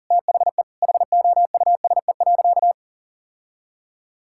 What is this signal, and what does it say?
Morse code, heard as a keyed tone, about 40 wpm, 700 Hz. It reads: THE HOUSE.
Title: Sentences from Top 1000 Words - Review Part 5 - 40wpm